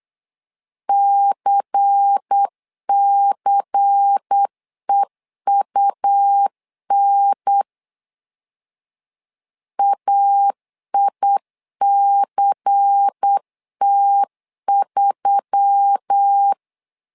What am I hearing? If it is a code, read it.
Morse code, heard as a keyed tone, about 8 wpm, 800 Hz. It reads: CCEUN AICT3
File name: LEZIONI / 5 LBVC / 5 LBVC 8wpm 2935